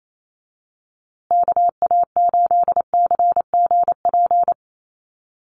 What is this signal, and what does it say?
Morse code, heard as a keyed tone, about 28 wpm, 700 Hz. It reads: KA8CGP
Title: USA Call Signs Repeated 2x 28wpm